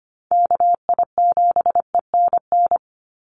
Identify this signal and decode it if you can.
Morse code, heard as a keyed tone, about 25 wpm, 700 Hz. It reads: KI7ENN
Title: USA Call Signs 25wpm